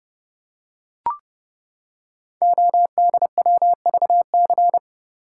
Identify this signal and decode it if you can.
Morse code, heard as a keyed tone, about 28 wpm, 700 Hz. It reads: ODWVC
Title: Code Groups - 5 Letters - 30wpm